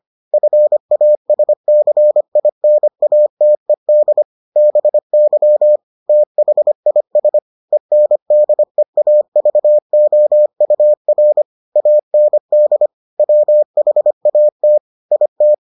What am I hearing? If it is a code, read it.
Morse code, heard as a keyed tone, about 25 wpm, 600 Hz. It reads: FASCINATED BY THIS ENDEAVOUR AND WHAT IT